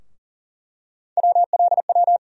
Morse code, heard as a keyed tone, about 40 wpm, 700 Hz. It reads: WLW
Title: USA Call Sign Postfixes - 3 Letters 40wpm